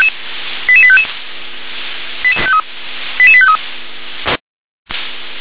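Five-tone selective call (ZVEI), heard as quick runs of five tones, tones of about 70 ms. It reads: 88855 88853
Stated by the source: ZVEI five-tone call 88148 received by SDR.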